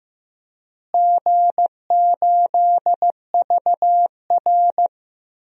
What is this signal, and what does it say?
Morse code, heard as a keyed tone, about 15 wpm, 700 Hz. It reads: G8VR
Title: DXCC-100 Call Signs - 4 Characters - Repeated 3x - 15wpm